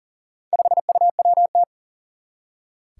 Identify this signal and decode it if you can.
Morse code, heard as a keyed tone, about 40 wpm, 700 Hz. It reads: 5UWT
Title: Four Challenging Letter-Number Combinations 40wpm